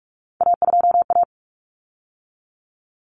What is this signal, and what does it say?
Morse code, heard as a keyed tone, about 44 wpm, 700 Hz. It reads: A2A
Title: Common Cut Numbers - 3-Digit 45wpm